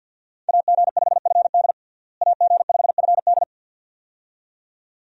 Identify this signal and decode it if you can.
Morse code, heard as a keyed tone, about 46 wpm, 700 Hz. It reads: AG5FD AG5FD
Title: USA Call Signs - 5 Characters - Repeated 2x - 50wpm